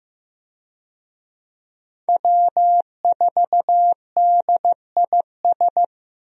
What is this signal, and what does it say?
Morse code, heard as a keyed tone, about 15 wpm, 700 Hz. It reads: W4DIS
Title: USA Call Signs - 5 Characters - Repeated 3x - 15wpm